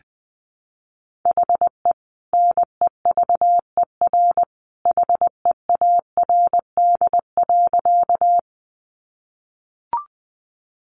Morse code, heard as a keyed tone, about 20 wpm, 700 Hz. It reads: HE NEVER HEARD.
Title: Sentences from Top 400 Words 20wpm